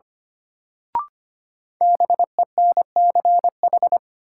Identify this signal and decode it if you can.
Morse code, heard as a keyed tone, about 25 wpm, 700 Hz. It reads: BENCH
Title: Sentences from Top 1400 Words - New Word Drill Alt - 25wpm